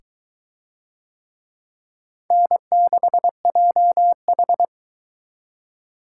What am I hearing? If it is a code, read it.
Morse code, heard as a keyed tone, about 22 wpm, 700 Hz. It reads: N6JH